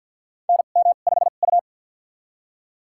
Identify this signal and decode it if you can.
Morse code, heard as a keyed tone, about 53 wpm, spaced as for 37 wpm, 700 Hz. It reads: NM5U
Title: USA Call Signs - 4 Characters - 2x Character Spacing - 50wpm